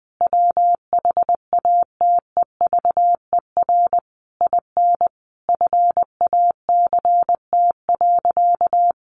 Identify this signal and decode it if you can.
Morse code, heard as a keyed tone, about 20 wpm, 700 Hz. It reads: WHATEVER IN FACT.
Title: Sentences from Top 1200 Words 20wpm